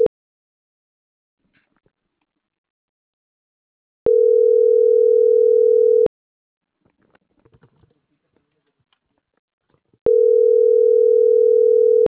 Ringback tone (North America), 440 + 480 Hz.